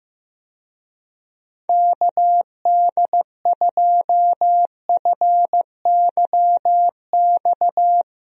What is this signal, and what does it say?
Morse code, heard as a keyed tone, about 16 wpm, 700 Hz. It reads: KD2FYX
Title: USA Call Signs - 6 Characters - Repeated 2x - 15wpm